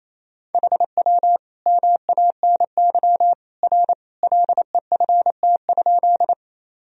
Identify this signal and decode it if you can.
Morse code, heard as a keyed tone, about 27 wpm, 700 Hz. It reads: HW MANY R LEFT?
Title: Sentences from Top 300 Words and Abbreviations 28wpm